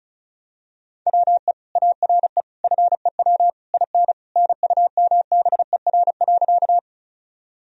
Morse code, heard as a keyed tone, about 36 wpm, 700 Hz. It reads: WE ARE FEW IN NUMBER.
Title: Sentences from Top 1000 Words - Review Part 1 - 35wpm